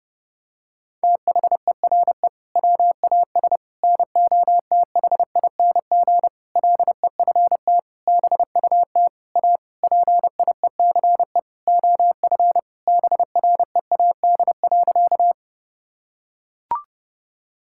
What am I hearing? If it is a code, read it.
Morse code, heard as a keyed tone, about 30 wpm, 700 Hz. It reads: THERE WAS NOTHING LEFT BUT A PIECE OF BREAD.
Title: Sentences from Top 1600 Words 30wpm